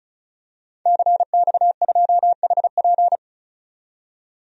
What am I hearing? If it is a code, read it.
Morse code, heard as a keyed tone, about 36 wpm, 700 Hz. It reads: CX2HP
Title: DXCC-100 Call Signs - 5 Characters - 35wpm